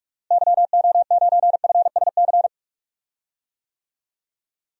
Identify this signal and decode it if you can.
Morse code, heard as a keyed tone, about 45 wpm, 700 Hz. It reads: YO9FSC
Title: DXCC-100 Call Signs - 6 Characters - Repeated 2x - 45wpm